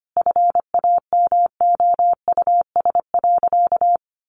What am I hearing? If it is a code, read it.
Morse code, heard as a keyed tone, about 25 wpm, 700 Hz. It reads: FAMOUS.
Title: Sentences from Top 1500 Words 25wpm